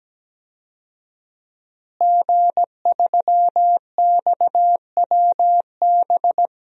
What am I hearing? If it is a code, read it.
Morse code, heard as a keyed tone, about 17 wpm, 700 Hz. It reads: G3XWB
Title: DXCC-100 Call Signs - 5 Characters - Repeated 3x - 17wpm